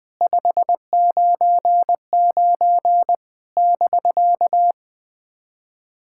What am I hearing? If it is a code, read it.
Morse code, heard as a keyed tone, about 20 wpm, 700 Hz. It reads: 599 <BK>
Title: SOTA QSO Practice 20wpm